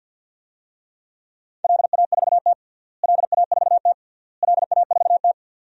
Morse code, heard as a keyed tone, about 48 wpm, 700 Hz. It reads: LA4T LA4T LA4T
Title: DXCC-100 Call Signs - 4 Characters - Repeated 3x - 50wpm